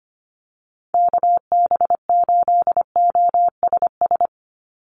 Morse code, heard as a keyed tone, about 25 wpm, 700 Hz. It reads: KB8OSS
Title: USA Call Signs 25wpm